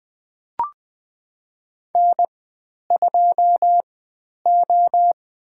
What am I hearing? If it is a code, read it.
Morse code, heard as a keyed tone, about 20 wpm, spaced as for 10 wpm, 700 Hz. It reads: N2O
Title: 102 Lesson 17g - Letter u - Call Signs 20wpm